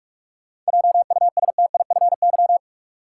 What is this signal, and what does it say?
Morse code, heard as a keyed tone, about 46 wpm, 700 Hz. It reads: JUSTIFY
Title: Seven Letter Words - 45wpm